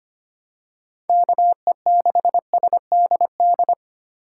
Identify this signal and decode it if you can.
Morse code, heard as a keyed tone, about 25 wpm, 700 Hz. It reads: KE6SDD
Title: USA Call Signs - 6 Characters - 3x Character Spacing - 25wpm